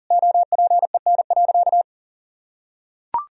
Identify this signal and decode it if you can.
Morse code, heard as a keyed tone, about 40 wpm, 700 Hz. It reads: OPEN.
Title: Sentences from Top 1000 Words - Review Part 1 - 40wpm